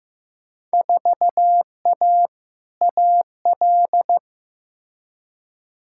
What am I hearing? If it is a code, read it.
Morse code, heard as a keyed tone, about 16 wpm, 700 Hz. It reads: 4A AL
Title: Field Day Exchanges - 15wpm